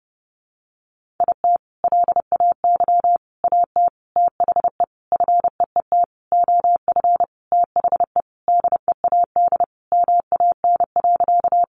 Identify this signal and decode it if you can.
Morse code, heard as a keyed tone, about 30 wpm, 700 Hz. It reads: IT LAY AT THE FEET OF THE DEAD MAN.
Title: Sentences from Top 1500 Words 30wpm